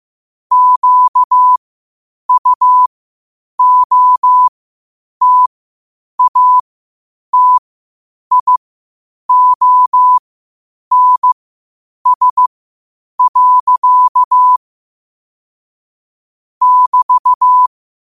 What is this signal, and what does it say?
Morse code, heard as a keyed tone, about 15 wpm, spaced as for 8 wpm, 1000 Hz. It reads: QUOTATIONS. <BT>
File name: qotdfiles / 8wpm